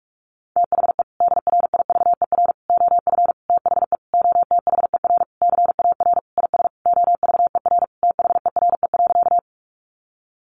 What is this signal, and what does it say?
Morse code, heard as a keyed tone, about 46 wpm, 700 Hz. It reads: THE DRIVER OF THE OTHER CAR IS OVER THERE.
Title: Sentences from Top 1000 Words 45wpm